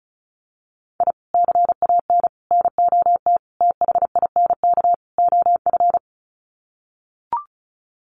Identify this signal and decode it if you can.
Morse code, heard as a keyed tone, about 34 wpm, 700 Hz. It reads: I CAN NOT THINK OF
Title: Sets of 5 Words 35wpm